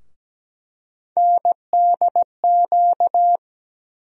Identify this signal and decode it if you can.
Morse code, heard as a keyed tone, about 17 wpm, 700 Hz. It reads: NDQ
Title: USA Call Sign Postfixes - 3 Letters 17wpm